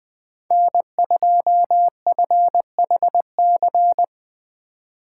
Morse code, heard as a keyed tone, about 20 wpm, 700 Hz. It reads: N2FHC
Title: USA Call Signs - 4x Character Spacing - 20wpm